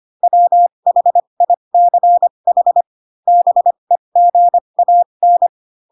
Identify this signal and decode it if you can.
Morse code, heard as a keyed tone, about 25 wpm, spaced as for 22 wpm, 700 Hz. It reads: WHICH BEGAN
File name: Pinocchio / Pinocchio ch3 18 25 ik1hge